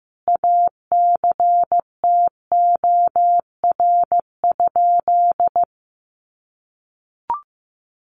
Morse code, heard as a keyed tone, about 15 wpm, 700 Hz. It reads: ACTOR?
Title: Sentences from Top 1500 Words 15wpm